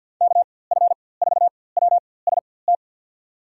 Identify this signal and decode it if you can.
Morse code, heard as a keyed tone, about 53 wpm, spaced as for 24 wpm, 700 Hz. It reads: KF4WST
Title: USA Call Signs - 6 Characters - 4x Character Spacing - 50wpm